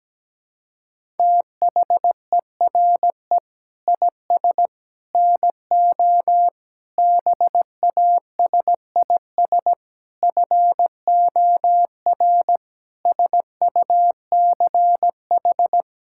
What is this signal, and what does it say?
Morse code, heard as a keyed tone, about 17 wpm, 700 Hz. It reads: THERE IS NO BASIS FOR SUCH
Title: Sentences from Top 1000 Words - Review Part 5 - 17wpm